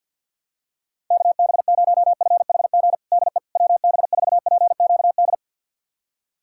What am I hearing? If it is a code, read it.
Morse code, heard as a keyed tone, about 51 wpm, 700 Hz. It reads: KB0FHG DE WB4PQD
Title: USA Callsign DE Callsign - 6 Characters - 50wpm